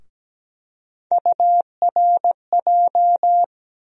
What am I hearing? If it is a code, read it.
Morse code, heard as a keyed tone, about 17 wpm, 700 Hz. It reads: URJ
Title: USA Call Sign Postfixes - 3 Letters 17wpm